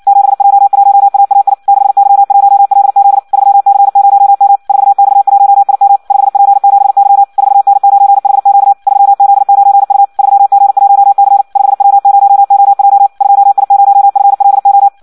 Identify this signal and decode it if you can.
Morse code, heard as a keyed tone, about 56 wpm, 800 Hz. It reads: 7J1AAI 6Y1LZ 3C0M 4X1IM 5P7Y 4N9HQ 3Z9U 3G1C 5W0OJ 2I0FLO
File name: Audio practica CW / qrq-tx-58 2